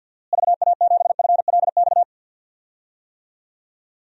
Morse code, heard as a keyed tone, about 47 wpm, 700 Hz. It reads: VA7FLX
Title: DXCC-100 Call Signs - 6 Characters - Repeated 2x - 50wpm